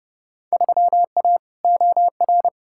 Morse code, heard as a keyed tone, about 29 wpm, 700 Hz. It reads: 3A OR
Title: Field Day Exchanges Encore - 30wpm